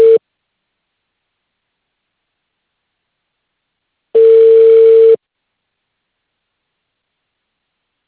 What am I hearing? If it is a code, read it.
Ringback tone (China, 450 Hz).